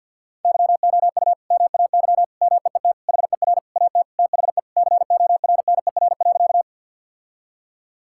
Morse code, heard as a keyed tone, about 50 wpm, 700 Hz. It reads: YOU MAY MEET HER AT THE CORNER.
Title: Sentences from Top 1600 Words 50wpm